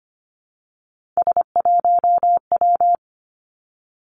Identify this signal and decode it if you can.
Morse code, heard as a keyed tone, about 25 wpm, 700 Hz. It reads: S1W
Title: Three Challenging Letter-Number Combinations 25wpm